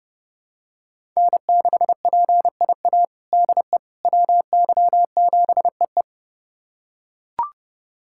Morse code, heard as a keyed tone, about 30 wpm, 700 Hz. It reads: N6PIA DE WY7EE
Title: USA Callsign DE Callsign - 5 Characters - 30wpm